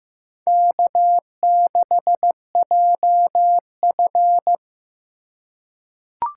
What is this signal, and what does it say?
Morse code, heard as a keyed tone, about 15 wpm, 700 Hz. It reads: K6JF
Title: USA Call Signs - 4 Characters - 3x Character Spacing - 15wpm